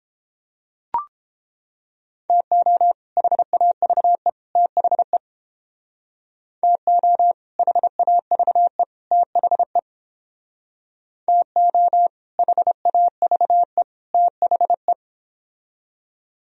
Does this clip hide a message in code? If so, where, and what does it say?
Yes: Morse code from 2.3 to 14.9 s, heard as a keyed tone, about 30 wpm, 700 Hz. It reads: TO HAVE THE TO HAVE THE TO HAVE THE